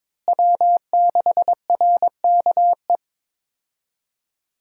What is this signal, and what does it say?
Morse code, heard as a keyed tone, about 22 wpm, 700 Hz. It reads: W6RKE
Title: USA Call Signs - 5 Characters - Repeated 3x - 22wpm